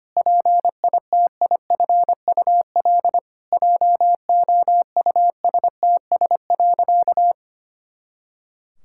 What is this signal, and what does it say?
Morse code, heard as a keyed tone, about 25 wpm, 700 Hz. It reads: PITIFUL JOUSTS.
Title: Pangram Challenge v2 - 25wpm